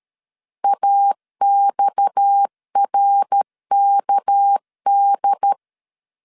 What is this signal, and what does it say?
Morse code, heard as a keyed tone, about 13 wpm, 800 Hz. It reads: AXRKD
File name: CW / DET / 1 6 1500C S B 12wpm 3431